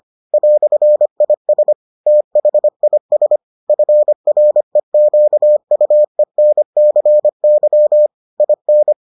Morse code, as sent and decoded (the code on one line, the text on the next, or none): .-..-. .. ... / - .... .. ... / ..-. .-. --.- ..- . -. -.-. -.-- / .. -.
"IS THIS FREQUENCY IN